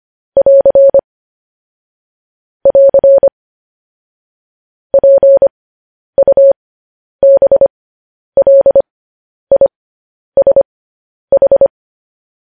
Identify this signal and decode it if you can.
Morse code, heard as a keyed tone, about 25 wpm, spaced as for 10 wpm, 550 Hz. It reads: <AR> <AR> PUBLISH
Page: news-headlines-in-morse-code-at-10-wpm